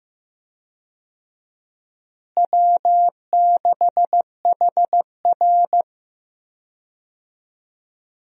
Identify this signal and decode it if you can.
Morse code, heard as a keyed tone, about 15 wpm, 700 Hz. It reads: W6HR
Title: USA Call Signs - 4 Characters - Repeated 3x - 15wpm